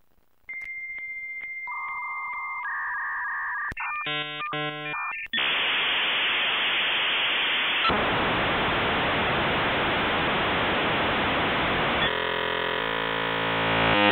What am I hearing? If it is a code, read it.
Dial-up modem handshake.
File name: The-Sound-of-dial-up-Internet